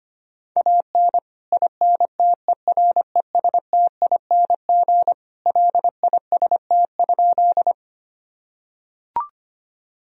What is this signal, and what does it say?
Morse code, heard as a keyed tone, about 25 wpm, 700 Hz. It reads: AN INTERESTING LIST?
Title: Sentences from Top 1000 Words - Review Part 2 - 25wpm